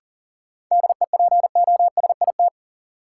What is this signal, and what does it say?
Morse code, heard as a keyed tone, about 40 wpm, 700 Hz. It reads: DEPOSIT
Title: Sentences from Top 1300 Words - New Word Drill Alt - 40wpm